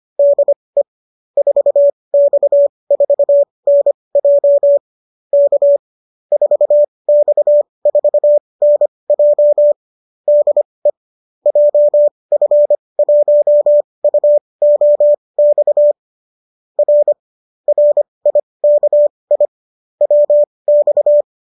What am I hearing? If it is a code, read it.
Morse code, heard as a keyed tone, about 25 wpm, spaced as for 20 wpm, 590 Hz. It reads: DE 4X4NJ K 4X4NJ DE JF1UOX R RIKI WX